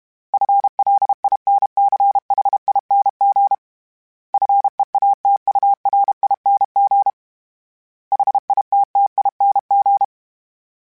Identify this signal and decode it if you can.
Morse code, heard as a keyed tone, about 32 wpm, 800 Hz. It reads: FLINCHING FEATURING HITTING
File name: Special INGs-mp3s / Special INGs-mp3s / Sngl Ing @30